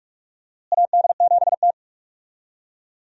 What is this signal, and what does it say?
Morse code, heard as a keyed tone, about 48 wpm, 700 Hz. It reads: AD7T
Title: USA Call Signs - 4 Characters - Repeated 2x - 45wpm